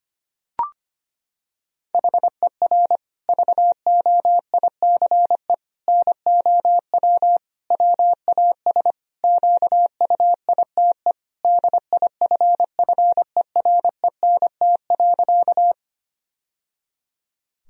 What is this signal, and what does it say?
Morse code, heard as a keyed tone, about 25 wpm, 700 Hz. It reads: HER VOICE NOW WAS QUITE DIFFERENT.
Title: Sentences from Top 1000 Words - Review Part 3 - 25wpm